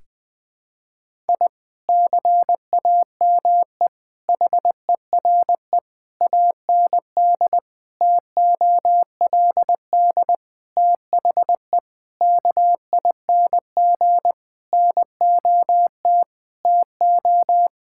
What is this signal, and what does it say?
Morse code, heard as a keyed tone, about 20 wpm, 700 Hz. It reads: I CAME HERE AND TOLD THE KING NOT TO
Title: Sentences from Top 1000 Words - Review Part 6 - 20wpm